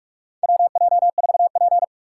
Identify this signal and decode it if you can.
Morse code, heard as a keyed tone, about 45 wpm, 700 Hz. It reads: WJ4P
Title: USA Call Signs - 4 Characters - 6x Character Spacing - 45wpm